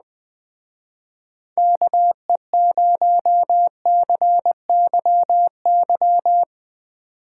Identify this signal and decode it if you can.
Morse code, heard as a keyed tone, about 21 wpm, 700 Hz. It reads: KE0CYY